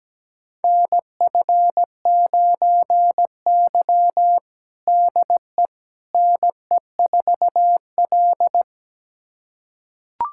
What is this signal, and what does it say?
Morse code, heard as a keyed tone, about 17 wpm, 700 Hz. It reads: NF9Y DE NE4L
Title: USA Callsign DE Callsign - 4 Characters - 17wpm